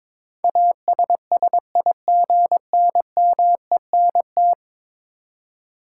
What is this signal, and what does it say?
Morse code, heard as a keyed tone, about 22 wpm, 700 Hz. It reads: ASSIGNMENT
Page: morse-code-ninja-practice-22wpm